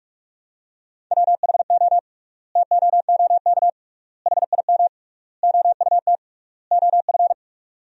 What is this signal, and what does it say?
Morse code, heard as a keyed tone, about 46 wpm, 700 Hz. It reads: WHO TOOK HIM OUT OF
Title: 3x Word-Spacing - Sentences from Top 300 Words - 45wpm